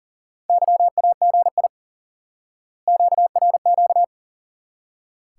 Morse code, heard as a keyed tone, about 40 wpm, 700 Hz. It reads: YAGI QRQ
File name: course / instant-qso-element-course-3 / IQE Course 3 - Lesson 89 - YAGI - Two Elements - 40wpm